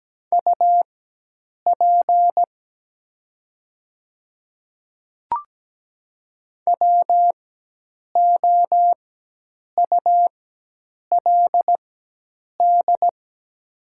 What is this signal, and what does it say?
Morse code, heard as a keyed tone, about 17 wpm, spaced as for 8 wpm, 700 Hz. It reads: UP WOULD
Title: 4x Character Spacing - Top 200 Words - 17wpm